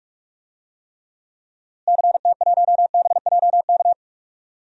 Morse code, heard as a keyed tone, about 47 wpm, 700 Hz. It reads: CT1BJK
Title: DXCC-100 Call Signs - 6 Characters - Repeated 2x - 45wpm